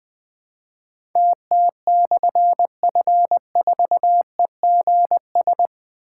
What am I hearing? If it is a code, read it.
Morse code, heard as a keyed tone, about 20 wpm, 700 Hz. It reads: TT/F4EGS
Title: Difficult Call Signs 20wpm